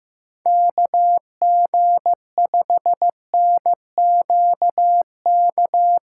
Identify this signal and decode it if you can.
Morse code, heard as a keyed tone, about 16 wpm, 700 Hz. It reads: KG5NQK
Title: USA Call Signs - 6 Characters - 2x Character Spacing - 15wpm